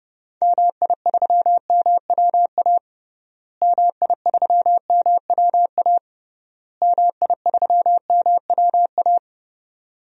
Morse code, heard as a keyed tone, about 30 wpm, 700 Hz. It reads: MI3MWA MI3MWA MI3MWA
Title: DXCC-100 Call Signs - 6 Characters - Repeated 3x - 30wpm